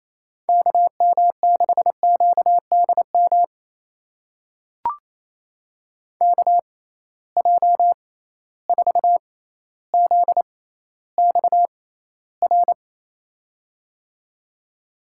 Morse code, heard as a keyed tone, about 27 wpm, spaced as for 14 wpm, 700 Hz. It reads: KM6QDM KJ4ZXR
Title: USA Call Signs - 6 Characters - 6x Character Spacing - 28wpm